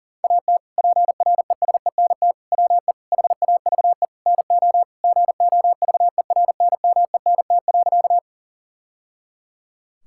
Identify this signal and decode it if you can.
Morse code, heard as a keyed tone, about 40 wpm, 700 Hz. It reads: AT PRESENT WE HAVE NO GOVERNMENT.
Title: Sentences from Top 1000 Words - Review Part 5 - 40wpm